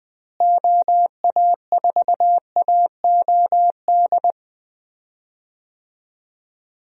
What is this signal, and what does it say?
Morse code, heard as a keyed tone, about 20 wpm, 700 Hz. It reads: OA4AOD
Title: USA Call Signs - 6 Characters - Repeated 2x - 20wpm